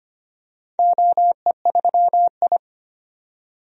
Morse code, heard as a keyed tone, about 25 wpm, 700 Hz. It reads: OE3I
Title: DXCC-100 Call Signs - 4 Characters - 25wpm